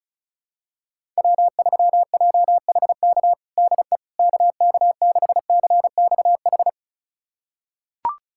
Morse code, heard as a keyed tone, about 36 wpm, 700 Hz. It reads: W3JHK DE KK6CXH
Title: USA Callsign DE Callsign 35wpm